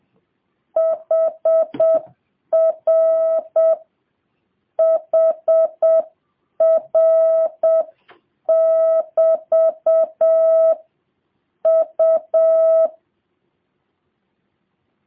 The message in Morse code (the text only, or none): HR HR <BT> U